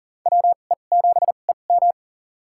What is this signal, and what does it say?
Morse code, heard as a keyed tone, about 40 wpm, spaced as for 29 wpm, 700 Hz. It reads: WE7EM
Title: USA Call Signs - 5 Characters - 2x Character Spacing - 40wpm